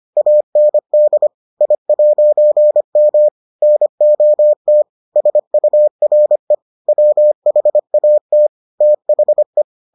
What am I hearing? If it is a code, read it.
Morse code, heard as a keyed tone, about 25 wpm, 600 Hz. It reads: AND I'M NOT SURE WHAT THE